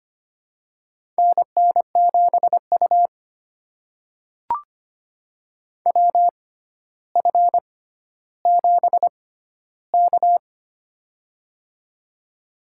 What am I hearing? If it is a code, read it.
Morse code, heard as a keyed tone, about 25 wpm, spaced as for 13 wpm, 700 Hz. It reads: NN7U WF7K
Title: USA Call Signs - 4 Characters - 6x Character Spacing - 25wpm